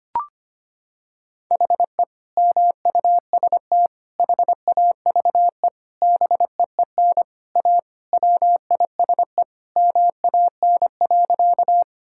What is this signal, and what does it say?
Morse code, heard as a keyed tone, about 25 wpm, 700 Hz. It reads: HE MUST HAVE BEEN A WISE MAN.
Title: Sentences from Top 1600 Words 25wpm